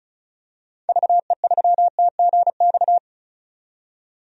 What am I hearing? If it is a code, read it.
Morse code, heard as a keyed tone, about 36 wpm, 700 Hz. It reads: VE3TGX
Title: DXCC-100 Call Signs - 6 Characters - 35wpm